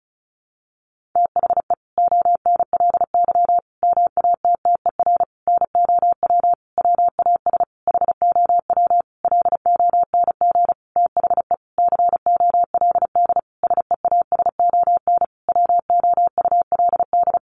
Morse code, heard as a keyed tone, about 34 wpm, 700 Hz. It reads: THE ONLY MATTER NOW WAS HOW LONG THE COLD SEASON WOULD
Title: Sentences from Top 1400 Words 35wpm